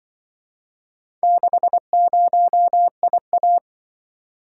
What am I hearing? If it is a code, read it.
Morse code, heard as a keyed tone, about 24 wpm, 700 Hz. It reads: 60IA
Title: Speed-Racing - Four Letter-Number Combinations - 20wpm